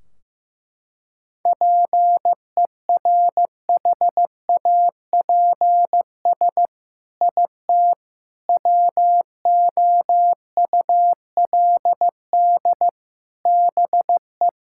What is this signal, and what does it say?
Morse code, heard as a keyed tone, about 15 wpm, 700 Hz. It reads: PERHAPS IT WOULD BE
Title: Sentences from Top 1600 Words 15wpm